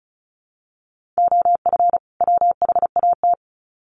Morse code, heard as a keyed tone, about 34 wpm, 700 Hz. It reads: OF WHAT
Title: Sets of 2 Words 35wpm